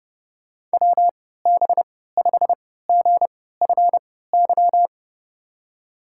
Morse code, heard as a keyed tone, about 31 wpm, spaced as for 17 wpm, 700 Hz. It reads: WB5GFY
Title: USA Call Signs - 3x Character Spacing - 30wpm